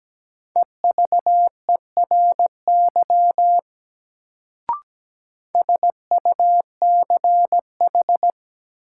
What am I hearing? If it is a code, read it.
Morse code, heard as a keyed tone, about 17 wpm, 700 Hz. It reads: EVERY SUCH
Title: Sentences from Top 200 Words - New Word Drill - 17wpm